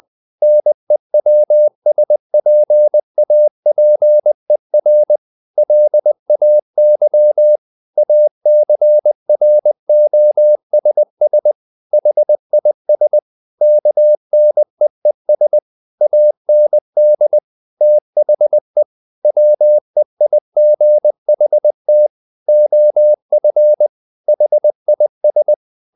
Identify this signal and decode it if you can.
Morse code, heard as a keyed tone, about 20 wpm, 600 Hz. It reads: NEWSPAPER LAY ACROSS HIS KNEES AND THE WEIGHT OF HIS
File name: the old man and the sea / the old man and the sea part 1 20wpm